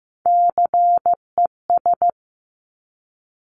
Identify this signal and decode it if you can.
Morse code, heard as a keyed tone, about 15 wpm, 700 Hz. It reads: CES
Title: Top Three Letter Combinations in Words 15wpm